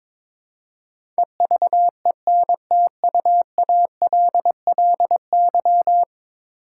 Morse code, heard as a keyed tone, about 22 wpm, 700 Hz. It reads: EVENTUALLY
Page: morse-code-ninja-practice-22wpm